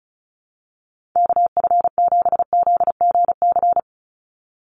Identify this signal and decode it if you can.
Morse code, heard as a keyed tone, about 34 wpm, 700 Hz. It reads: KF7ZGC
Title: USA Call Signs 35wpm